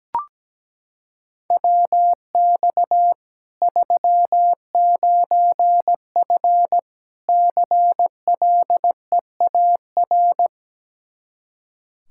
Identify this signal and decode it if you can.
Morse code, heard as a keyed tone, about 17 wpm, 700 Hz. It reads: WX 39F CLEAR
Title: Practice QSO - Exchange 2 - Info - 17wpm